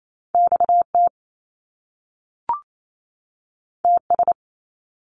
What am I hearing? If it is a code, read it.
Morse code, heard as a keyed tone, about 28 wpm, 700 Hz. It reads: XT TS
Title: Top Two Letter Combinations in Words 28wpm